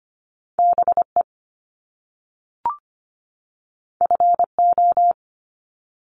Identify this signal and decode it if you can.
Morse code, heard as a keyed tone, about 25 wpm, 700 Hz. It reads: BE FO
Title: Top Two Letter Combinations in Words 25wpm